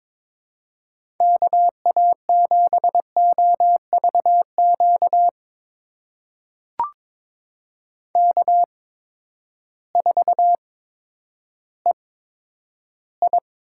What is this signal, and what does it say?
Morse code, heard as a keyed tone, about 21 wpm, spaced as for 11 wpm, 700 Hz. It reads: KA7OVQ K4EI
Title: USA Call Signs - 8x Character Spacing - 22wpm